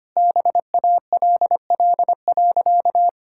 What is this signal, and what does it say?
Morse code, heard as a keyed tone, about 25 wpm, 690 Hz. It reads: BALL.
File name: Easy Sentences with Hints - 25wpm